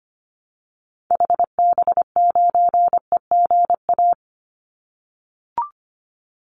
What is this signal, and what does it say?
Morse code, heard as a keyed tone, about 25 wpm, 700 Hz. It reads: HB9EGA
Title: Call Signs - 25wpm